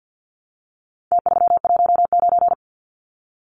Morse code, heard as a keyed tone, about 50 wpm, 700 Hz. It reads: T319
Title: Numbers - Contest Style - 4-Digit 50wpm